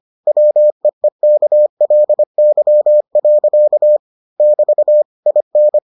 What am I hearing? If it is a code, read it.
Morse code, heard as a keyed tone, about 25 wpm, 600 Hz. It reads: WEEKLY. <BT> IN